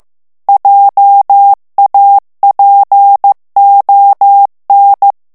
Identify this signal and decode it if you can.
Morse code, heard as a keyed tone, about 15 wpm, 800 Hz. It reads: JAPON
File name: morse mp3 / FRANCAIS TEXTE CLAIR 1 VITESSE 14 MOTS MINUTES